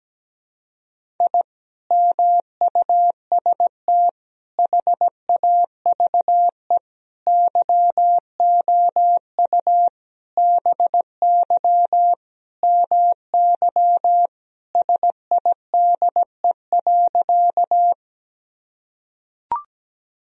Morse code, heard as a keyed tone, about 17 wpm, 700 Hz. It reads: I MUST HAVE YOU BY MY SIDE.
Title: Sentences from Top 1000 Words - Review Part 1 - 17wpm